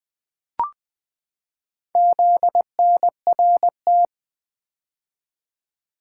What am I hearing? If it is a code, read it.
Morse code, heard as a keyed tone, about 20 wpm, 700 Hz. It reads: ZNRT
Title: Four Letter-Number Combinations - 20wpm